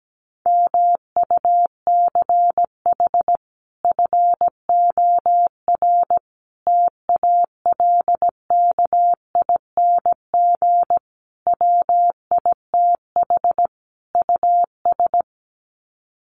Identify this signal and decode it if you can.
Morse code, heard as a keyed tone, about 17 wpm, 700 Hz. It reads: MUCH FOR TALKING WITH US
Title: Sets of 5 Words Encore 17wpm